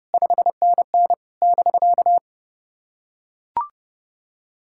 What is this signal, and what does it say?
Morse code, heard as a keyed tone, about 30 wpm, 690 Hz. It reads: 5NN <BK>
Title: SOTA QSO Practice 30wpm